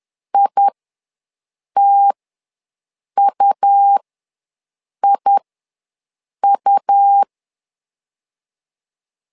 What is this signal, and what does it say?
Morse code, heard as a keyed tone, about 11 wpm, spaced as for 6 wpm, 800 Hz. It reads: ITUIU
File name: LEZIONI / 2 UHDN / 2 U ws 10wpm 2017